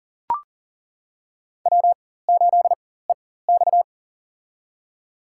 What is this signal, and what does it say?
Morse code, heard as a keyed tone, about 40 wpm, spaced as for 19 wpm, 700 Hz. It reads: W8EX
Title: USA Call Signs - 4 Characters - 4x Character Spacing - 40wpm